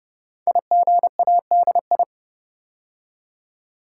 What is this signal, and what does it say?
Morse code, heard as a keyed tone, about 29 wpm, 700 Hz. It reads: IGADI
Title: Code Groups - 5 Letters - 30wpm